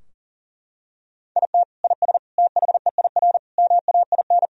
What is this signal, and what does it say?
Morse code, heard as a keyed tone, about 40 wpm, 700 Hz. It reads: IT IS THEIR MAIN
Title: Sentences from Top 1000 Words - Review Part 2 - 40wpm